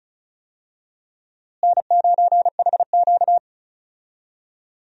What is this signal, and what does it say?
Morse code, heard as a keyed tone, about 36 wpm, 700 Hz. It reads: N9HQ
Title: USA Call Signs - 4 Characters - Repeated 2x - 35wpm